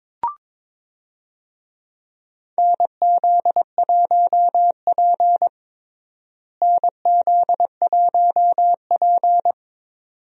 Morse code, heard as a keyed tone, about 22 wpm, 700 Hz. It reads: NZ1P NZ1P
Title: USA Call Signs - 4 Characters - Repeated 2x - 22wpm